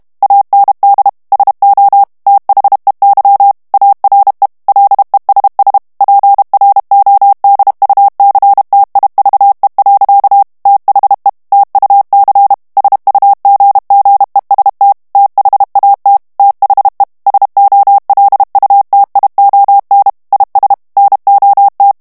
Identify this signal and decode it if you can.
Morse code, heard as a keyed tone, about 32 wpm, 800 Hz. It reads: AND SO THEY ARE LESS PRODUCTIVE. THE TUC SUGGEST THAT THE SOLUTION IS NOT
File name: morse mp3 / ANGLAIS texte clair 1 vitesse 30 mots minute